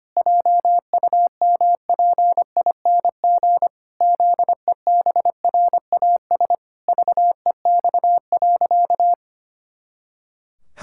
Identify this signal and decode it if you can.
Morse code, heard as a keyed tone, about 25 wpm, 700 Hz. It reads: JUMPING ZEBRAS VEX.
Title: Pangram Challenge v2 - 25wpm